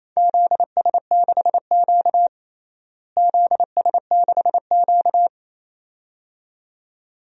Morse code, heard as a keyed tone, about 28 wpm, 700 Hz. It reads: ZS6Q ZS6Q
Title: DXCC-100 Call Signs - 4 Characters - Repeated 2x - 28wpm